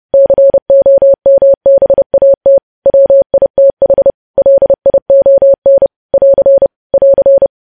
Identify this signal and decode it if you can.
Morse code, heard as a keyed tone, about 30 wpm, 550 Hz. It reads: COMBAT WITH LION <AR> <AR>
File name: podcast / News headlines-1x-30x30